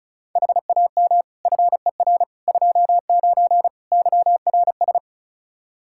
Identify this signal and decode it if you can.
Morse code, heard as a keyed tone, about 36 wpm, 700 Hz. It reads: HAM FER 29 YRS
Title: Practice QSO - Exchange 3 - More Info - 35wpm